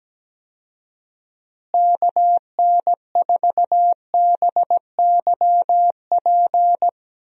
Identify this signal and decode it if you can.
Morse code, heard as a keyed tone, about 17 wpm, 700 Hz. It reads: KN4BYP
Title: USA Call Signs - 6 Characters - Repeated 2x - 17wpm